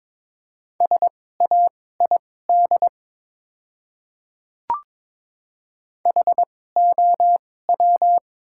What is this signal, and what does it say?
Morse code, heard as a keyed tone, about 22 wpm, spaced as for 16 wpm, 700 Hz. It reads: SAID HOW
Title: 2x Character Spacing - Top 100 Words - 22wpm